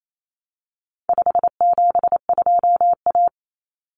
Morse code, heard as a keyed tone, about 28 wpm, 700 Hz. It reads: 572A
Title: Common Cut Numbers - 4-Digit 28wpm